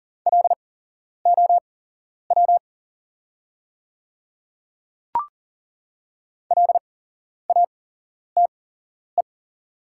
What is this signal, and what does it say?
Morse code, heard as a keyed tone, about 40 wpm, spaced as for 11 wpm, 700 Hz. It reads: LOW LATE
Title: 8x Character Spacing - Top 500 Words - 40wpm